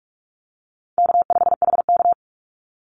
Morse code, heard as a keyed tone, about 44 wpm, 700 Hz. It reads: K5HK
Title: Call Signs Repeated 2x - 45wpm